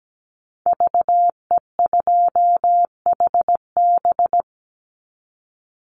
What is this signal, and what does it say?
Morse code, heard as a keyed tone, about 17 wpm, 700 Hz. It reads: VE2HB
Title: Call Signs - 17wpm